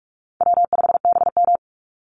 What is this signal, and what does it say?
Morse code, heard as a keyed tone, about 44 wpm, 700 Hz. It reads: W5BM
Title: Easy Call Signs - 45wpm